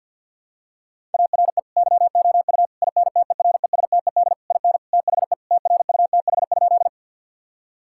Morse code, heard as a keyed tone, about 50 wpm, 700 Hz. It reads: ARE YOU INTERESTED IN THE TRUTH?
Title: Sentences from Top 1000 Words - Review Part 6 - 50wpm